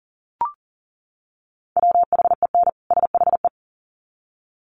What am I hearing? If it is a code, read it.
Morse code, heard as a keyed tone, about 40 wpm, 700 Hz. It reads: WHEN SHE
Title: Sets of 2 Words 40wpm